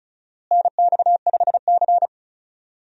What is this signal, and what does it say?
Morse code, heard as a keyed tone, about 36 wpm, 700 Hz. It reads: NX5C
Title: USA Call Signs - 4 Characters - Repeated 2x - 35wpm